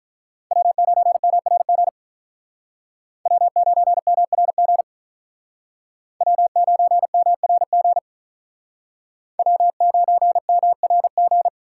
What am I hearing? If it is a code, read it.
Morse code, heard as a keyed tone, about 46 wpm, 700 Hz. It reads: W9MRG W9MRG W9MRG W9MRG